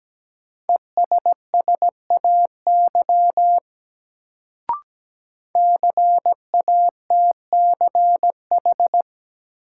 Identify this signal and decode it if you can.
Morse code, heard as a keyed tone, about 17 wpm, 700 Hz. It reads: ESSAY CATCH
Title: Sentences from Top 1500 Words - New Word Drill - 17wpm